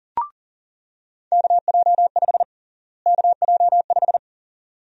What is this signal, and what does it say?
Morse code, heard as a keyed tone, about 40 wpm, 700 Hz. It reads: KJ5 KJ5
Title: USA Call Sign Prefixes - Repeated 2x - 40wpm